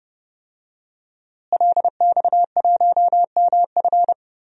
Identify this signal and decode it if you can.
Morse code, heard as a keyed tone, about 30 wpm, 700 Hz. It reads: LX1MF
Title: DXCC-100 Call Signs - 5 Characters - Repeated 2x - 30wpm